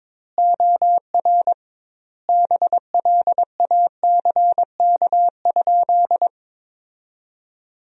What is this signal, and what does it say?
Morse code, heard as a keyed tone, about 22 wpm, 700 Hz. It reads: OR BLACK?
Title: 2x Word-Spacing - Sentences from Top 400 Words - 22wpm